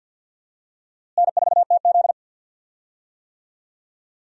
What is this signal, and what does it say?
Morse code, heard as a keyed tone, about 50 wpm, 700 Hz. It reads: N4TZ
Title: USA Call Signs - 4 Characters - Repeated 2x - 50wpm